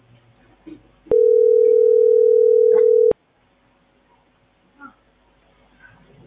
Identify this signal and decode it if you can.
Ringback tone (North America), 440 + 480 Hz.